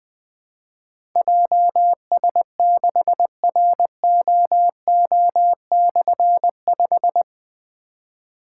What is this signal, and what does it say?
Morse code, heard as a keyed tone, about 20 wpm, 700 Hz. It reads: JS6ROO/5
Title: Difficult Call Signs 20wpm